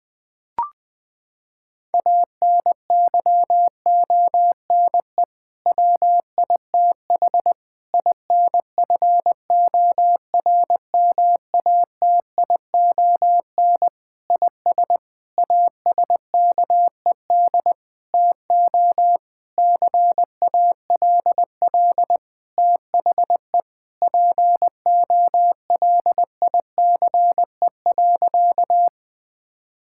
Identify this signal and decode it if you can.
Morse code, heard as a keyed tone, about 20 wpm, 700 Hz. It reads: ANYONE WITH INFORMATION IS ASKED TO CALL THE POLICE.
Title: Sentences with 6-Word Phrases - 20wpm